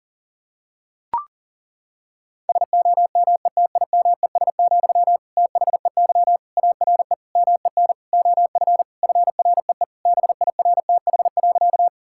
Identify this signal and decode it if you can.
Morse code, heard as a keyed tone, about 37 wpm, 700 Hz. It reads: SOMETIMES, THEY ARE MEN OF FREE BIRTH.
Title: Sentences from Top 1000 Words - Review Part 5 - 40wpm